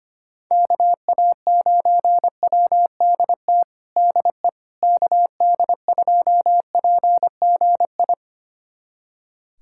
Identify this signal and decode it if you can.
Morse code, heard as a keyed tone, about 25 wpm, 700 Hz. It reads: KA9WDT DE KD2PGI
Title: USA Callsign DE Callsign - 6 Characters - 25wpm